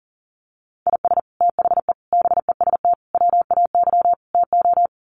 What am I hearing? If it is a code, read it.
Morse code, heard as a keyed tone, about 37 wpm, 700 Hz. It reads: IS THE BEST WAY TO
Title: Sets of 5 Words Encore 40wpm